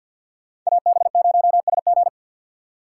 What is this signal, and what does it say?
Morse code, heard as a keyed tone, about 51 wpm, 690 Hz. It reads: AB0SG
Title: USA Call Signs - 5 Characters - 6x Character Spacing - 50wpm